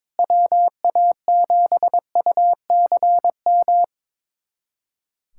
Morse code, heard as a keyed tone, about 22 wpm, 700 Hz. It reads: WA7UCM
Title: USA Callsign DE Callsign - 6 Characters - 22wpm